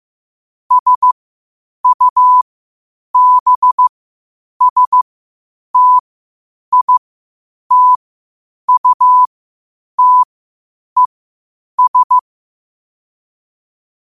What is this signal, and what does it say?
Morse code, heard as a keyed tone, about 15 wpm, spaced as for 8 wpm, 1000 Hz. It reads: SUBSTITUTES
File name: qotdfiles / 8wpm